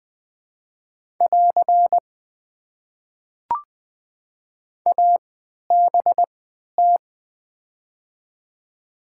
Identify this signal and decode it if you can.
Morse code, heard as a keyed tone, about 21 wpm, spaced as for 11 wpm, 700 Hz. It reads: <AR> ABT
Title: Top 90 QSO Elements - 3x Character Spacing - 20wpm